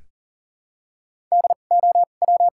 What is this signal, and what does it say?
Morse code, heard as a keyed tone, about 40 wpm, spaced as for 29 wpm, 700 Hz. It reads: DOW
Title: 2x Character Spacing - Top 100 Words - 40wpm